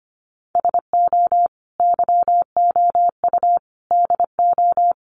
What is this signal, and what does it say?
Morse code, heard as a keyed tone, about 25 wpm, 700 Hz. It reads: SO YOU DO
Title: Sets of 5 Words Encore 25wpm